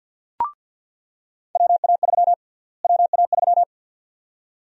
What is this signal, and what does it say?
Morse code, heard as a keyed tone, about 47 wpm, 700 Hz. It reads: WA3 WA3
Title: USA Call Sign Prefixes - Repeated 2x - 50wpm